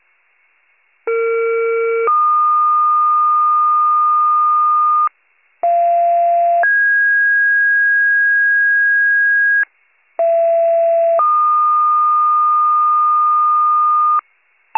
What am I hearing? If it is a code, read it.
Two-tone page, heard as a one-second tone, then a three-second tone, about 460 then 1220 Hz, about 690 then 1650 Hz, about 670 then 1220 Hz. It is heard from 1.1 to 14.2 s.